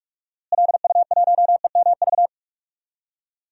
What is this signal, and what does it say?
Morse code, heard as a keyed tone, about 42 wpm, 700 Hz. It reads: LU1EMV
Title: DXCC-100 Call Signs - 6 Characters - Repeated 2x - 45wpm